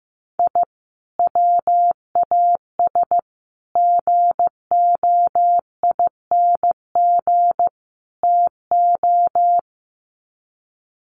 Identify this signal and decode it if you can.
Morse code, heard as a keyed tone, about 15 wpm, 700 Hz. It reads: I WAS GOING TO
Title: Sets of 4 Words 15wpm